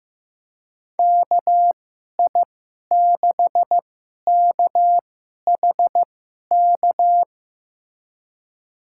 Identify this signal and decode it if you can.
Morse code, heard as a keyed tone, about 15 wpm, spaced as for 11 wpm, 700 Hz. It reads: KI6KHK
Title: USA Call Signs - 6 Characters - 2x Character Spacing - 15wpm